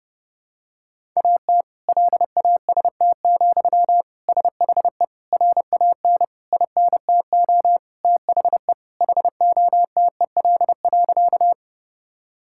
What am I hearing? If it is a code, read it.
Morse code, heard as a keyed tone, about 30 wpm, 700 Hz. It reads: AT LAST, SHE RAN INTO THE HOTEL.
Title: Sentences from Top 1600 Words 30wpm